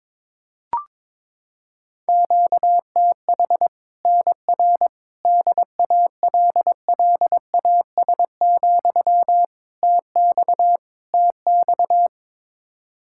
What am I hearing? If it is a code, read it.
Morse code, heard as a keyed tone, about 22 wpm, 700 Hz. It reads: QTH NR DALLAS, TX TX
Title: Practice QSO - Exchange 1 - Beginning - 22wpm